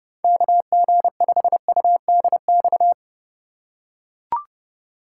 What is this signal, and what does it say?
Morse code, heard as a keyed tone, about 30 wpm, 700 Hz. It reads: KG5UDX
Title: USA Call Signs - 6 Characters - 4x Character Spacing - 30wpm